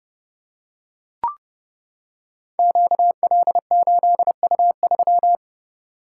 Morse code, heard as a keyed tone, about 30 wpm, 700 Hz. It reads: QL8U3
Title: Code Groups - 5 Letter:Numbers - 30wpm